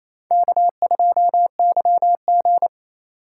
Morse code, heard as a keyed tone, about 28 wpm, 700 Hz. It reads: K2YG
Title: USA Call Signs - 4 Characters - 3x Character Spacing - 28wpm